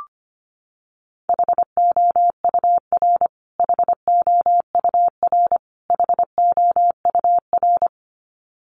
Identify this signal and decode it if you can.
Morse code, heard as a keyed tone, about 25 wpm, 700 Hz. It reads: HOUR HOUR HOUR
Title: Top 500 Words Repeated 25wpm